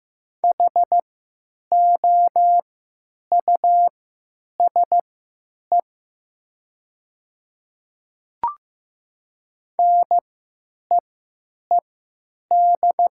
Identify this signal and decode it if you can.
Morse code, heard as a keyed tone, about 15 wpm, spaced as for 9 wpm, 700 Hz. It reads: HOUSE NEED
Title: 3x Character Spacing - Top 500 Words - 15wpm